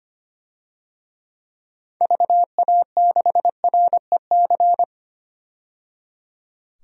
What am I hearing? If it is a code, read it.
Morse code, heard as a keyed tone, about 25 wpm, 700 Hz. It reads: VA6REC
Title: DXCC-100 Call Signs - 6 Characters - 25wpm